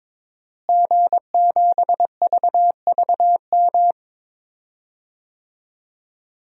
Morse code, heard as a keyed tone, about 22 wpm, 700 Hz. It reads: G7VVM
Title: DXCC-100 Call Signs - 5 Characters - Repeated 3x - 22wpm